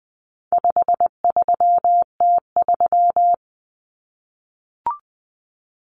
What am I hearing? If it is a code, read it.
Morse code, heard as a keyed tone, about 20 wpm, 700 Hz. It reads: 53T3
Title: Common Cut Numbers - 4-Digit 20wpm